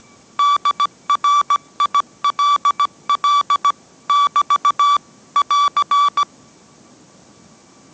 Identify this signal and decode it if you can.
Morse code, heard as a keyed tone, about 17 wpm, 1150 Hz. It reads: DRILL <BT> <AR>